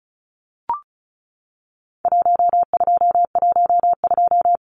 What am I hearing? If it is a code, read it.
Morse code, heard as a keyed tone, about 34 wpm, 700 Hz. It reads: 1212
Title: Numbers - Contest Style - 4-Digit 35wpm